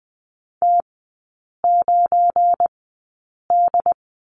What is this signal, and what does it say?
Morse code, heard as a keyed tone, about 20 wpm, spaced as for 8 wpm, 700 Hz. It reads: T9D
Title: 218 Lesson 35e - Number 0 - Three Characters 20wpm